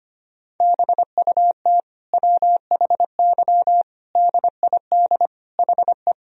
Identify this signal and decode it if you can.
Morse code, heard as a keyed tone, about 25 wpm, 700 Hz. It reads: BUT WHY DID HE
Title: Sentences from Top 1000 Words - Review Part 2 - 25wpm